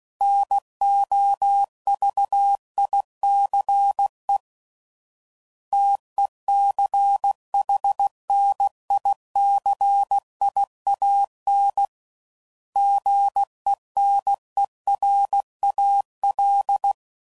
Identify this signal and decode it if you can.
Morse code, heard as a keyed tone, about 16 wpm, 800 Hz. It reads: NOVICE TECHNICIAN GENERAL
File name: CW Code Groups @15WPM / fromSM's / Classes